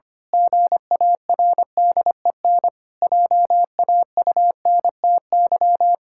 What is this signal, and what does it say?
Morse code, heard as a keyed tone, about 25 wpm, 700 Hz. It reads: GARDEN JAUNTY
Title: Pangram Challenge v2 - 25wpm